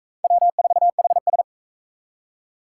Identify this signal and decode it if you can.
Morse code, heard as a keyed tone, about 38 wpm, 700 Hz. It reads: W4HS